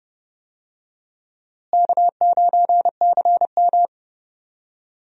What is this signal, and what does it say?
Morse code, heard as a keyed tone, about 31 wpm, 700 Hz. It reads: K9CM
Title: USA Call Signs - 4 Characters - Repeated 2x - 30wpm